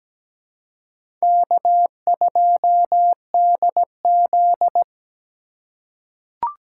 Morse code, heard as a keyed tone, about 17 wpm, 700 Hz. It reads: K2DZ
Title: USA Call Signs - 4x Character Spacing - 17wpm